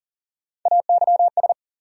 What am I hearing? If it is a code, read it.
Morse code, heard as a keyed tone, about 40 wpm, 700 Hz. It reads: AYS
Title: USA Call Sign Postfixes - 3 Letters 40wpm